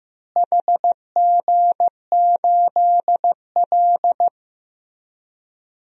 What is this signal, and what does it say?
Morse code, heard as a keyed tone, about 15 wpm, 700 Hz. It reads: HG8L
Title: DXCC-100 Call Signs - 4 Characters - Repeated 3x - 15wpm